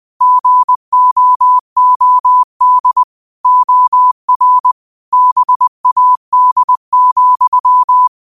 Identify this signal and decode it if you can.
Morse code, heard as a keyed tone, about 20 wpm, 1000 Hz. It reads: GOOD OR BAD,
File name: qotdfiles / 20wpm